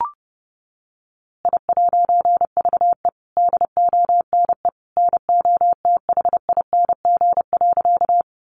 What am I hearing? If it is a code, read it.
Morse code, heard as a keyed tone, about 30 wpm, 700 Hz. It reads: I'VE DONE NOTHING.
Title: A Sound Of Thunder 30wpm